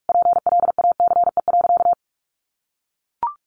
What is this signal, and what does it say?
Morse code, heard as a keyed tone, about 44 wpm, 700 Hz. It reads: PLACE.
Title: Additional Easy Phrases 45wpm